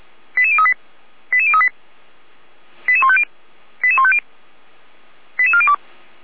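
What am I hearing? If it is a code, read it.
Five-tone selective call (ZVEI), heard as quick runs of five tones, tones of about 70 ms. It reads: 88038 88038 88150 88150 88402
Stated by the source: ZVEI five-tone call received by SDR.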